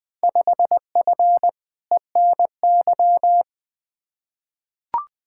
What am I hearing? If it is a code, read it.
Morse code, heard as a keyed tone, about 20 wpm, 700 Hz. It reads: 5F ENY
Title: Field Day Exchanges Encore - 20wpm